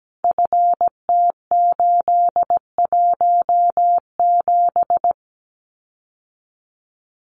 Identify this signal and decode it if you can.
Morse code, heard as a keyed tone, about 17 wpm, 700 Hz. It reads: FT817
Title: Common QSO Phrases Encore 17wpm